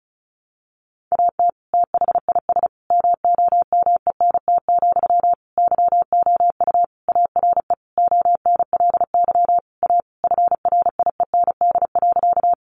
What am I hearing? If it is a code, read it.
Morse code, heard as a keyed tone, about 34 wpm, 700 Hz. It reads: AT THIS MOMENT, YOU ARE ONLY A FRIEND.
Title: Sentences from Top 1000 Words 35wpm